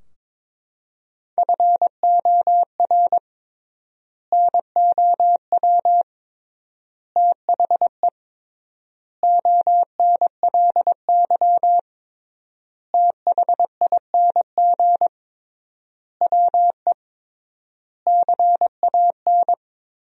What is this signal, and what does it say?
Morse code, heard as a keyed tone, about 22 wpm, 700 Hz. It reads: FOR NOW THE ONLY THING WE CAN
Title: 3x Word-Spacing - Sentences from Top 200 Words - 22wpm